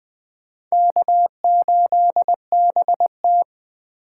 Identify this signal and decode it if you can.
Morse code, heard as a keyed tone, about 24 wpm, 700 Hz. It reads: K8BT
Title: Speed Racing - USA 1x2 Call Signs 20wpm